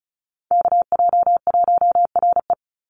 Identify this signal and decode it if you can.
Morse code, heard as a keyed tone, about 34 wpm, 700 Hz. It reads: KJ1RE
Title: Call Signs Repeated 3x - 35wpm